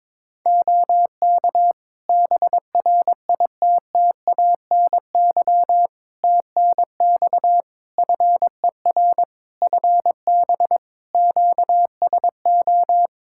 Morse code, heard as a keyed tone, about 22 wpm, 700 Hz. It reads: OK BRITTANY TNX FER FB QSO
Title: Practice QSO - Exchange 4 - Ending - 22wpm